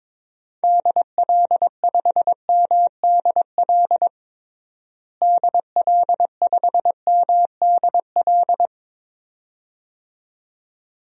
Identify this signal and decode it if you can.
Morse code, heard as a keyed tone, about 22 wpm, 700 Hz. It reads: DL5MDL DL5MDL
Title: DXCC-100 Call Signs - 6 Characters - Repeated 2x - 22wpm